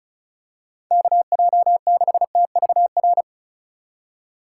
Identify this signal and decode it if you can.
Morse code, heard as a keyed tone, about 34 wpm, 700 Hz. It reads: KJ6TVR
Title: USA Call Signs - 6 Characters - 2x Character Spacing - 35wpm